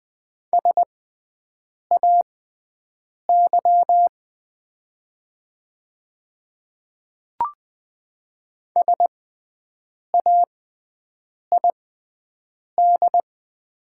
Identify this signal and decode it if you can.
Morse code, heard as a keyed tone, about 20 wpm, spaced as for 7 wpm, 690 Hz. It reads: SAY SAID
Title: 6x Character Spacing - Top 300 Words - 20wpm